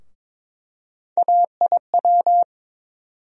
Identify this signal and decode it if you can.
Morse code, heard as a keyed tone, about 22 wpm, 700 Hz. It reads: AIW
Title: USA Call Sign Postfixes - 3 Letters 22wpm